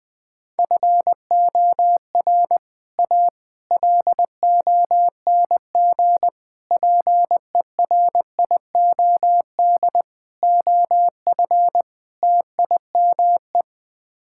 Morse code, heard as a keyed tone, about 20 wpm, 700 Hz. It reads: FOR A LONG PERIOD OF TIME
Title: Sets of 6 Words - 20wpm